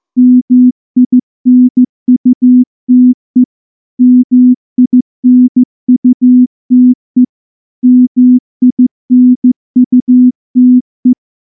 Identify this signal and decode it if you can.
Morse code, heard as a keyed tone, about 14 wpm, 250 Hz. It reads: MINUTE MINUTE MINUTE
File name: morse-training-15WPM